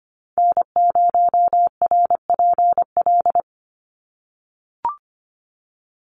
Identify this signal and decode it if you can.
Morse code, heard as a keyed tone, about 25 wpm, 700 Hz. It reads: N0RPL
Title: USA Call Signs 25wpm